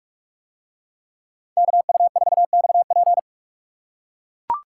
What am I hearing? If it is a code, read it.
Morse code, heard as a keyed tone, about 40 wpm, 700 Hz. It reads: KU4XP